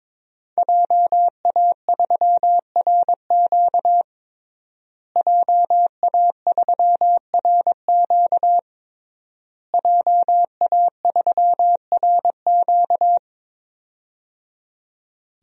Morse code, heard as a keyed tone, about 22 wpm, 700 Hz. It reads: JA3RQ JA3RQ JA3RQ
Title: DXCC-100 Call Signs - 5 Characters - Repeated 3x - 22wpm